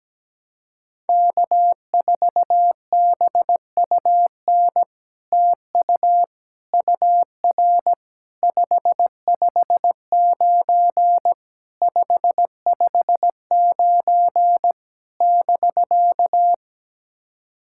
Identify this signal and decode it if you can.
Morse code, heard as a keyed tone, about 17 wpm, 700 Hz. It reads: K4BUN TU UR 559 559 <BK>
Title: SOTA QSO Practice 17wpm